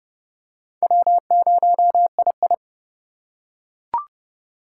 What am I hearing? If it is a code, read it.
Morse code, heard as a keyed tone, about 28 wpm, 700 Hz. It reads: W0II